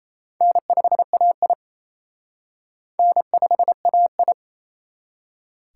Morse code, heard as a keyed tone, about 32 wpm, 700 Hz. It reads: N5AI N5AI
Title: Speed Racing - USA 1x2 Call Signs 28wpm